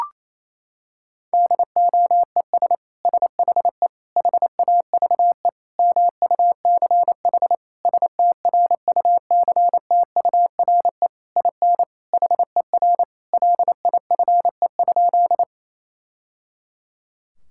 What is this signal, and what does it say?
Morse code, heard as a keyed tone, about 28 wpm, 700 Hz. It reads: DOES SHE HAVE MUCH STRUCTURE IN HER LIFE?
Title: Sentences from Top 1000 Words - Review Part 6 - 28wpm